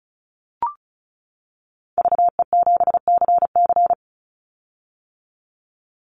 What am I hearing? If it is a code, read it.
Morse code, heard as a keyed tone, about 36 wpm, 700 Hz. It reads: VE7CC
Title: Call Signs - 35wpm